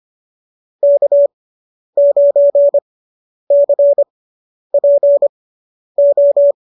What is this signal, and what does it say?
Morse code, heard as a keyed tone, about 25 wpm, spaced as for 10 wpm, 590 Hz. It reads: K9CPO